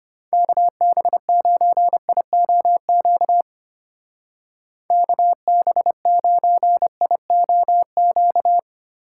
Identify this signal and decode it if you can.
Morse code, heard as a keyed tone, about 32 wpm, 700 Hz. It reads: KB9IOQ KB9IOQ